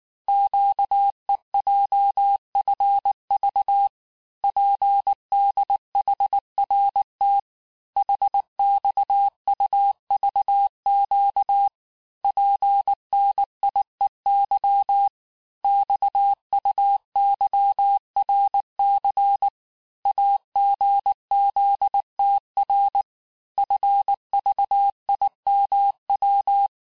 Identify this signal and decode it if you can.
Morse code, heard as a keyed tone, about 19 wpm, 800 Hz. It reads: QEJFV PDHRT HXUVQ PNIEY XUYRC AGZTR FVIMW
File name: cours-du-mercredi / audios / cours-869-texte-CHIFFRE-vitesse-18-MPM